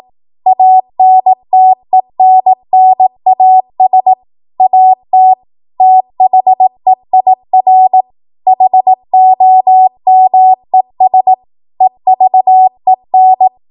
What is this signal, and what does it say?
Morse code, heard as a keyed tone, about 18 wpm, 750 Hz. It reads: ANTENNAS AT THEIR HOMES EVEN
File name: Morse / 18WPM